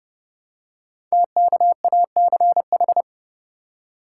Morse code, heard as a keyed tone, about 28 wpm, 700 Hz. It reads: TKACH
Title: Code Groups - 5 Letters - 30wpm